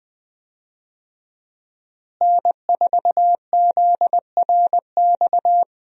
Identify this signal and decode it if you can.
Morse code, heard as a keyed tone, about 20 wpm, 700 Hz. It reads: N4ZRX
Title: USA Call Signs - 5 Characters - Repeated 3x - 20wpm